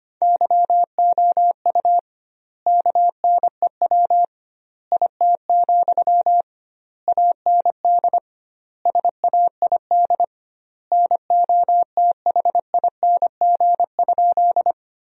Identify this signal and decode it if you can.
Morse code, heard as a keyed tone, about 25 wpm, 700 Hz. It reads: YOU KNEW IT, AND SAID NOTHING?
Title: 2x Word-Spacing - Sentences from Top 500 Words - 25wpm